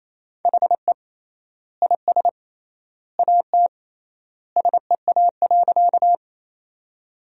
Morse code, heard as a keyed tone, about 28 wpm, 700 Hz. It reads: HE IS AT SEA.
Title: 3x Word-Spacing - Sentences from Top 300 Words - 28wpm